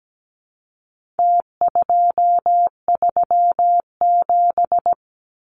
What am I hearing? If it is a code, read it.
Morse code, heard as a keyed tone, about 17 wpm, 700 Hz. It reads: T237
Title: Numbers - Contest Style - 4-Digit 17wpm